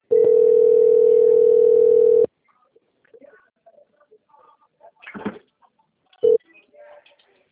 Ringback tone (North America), 440 + 480 Hz.
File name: out- - - - - .29482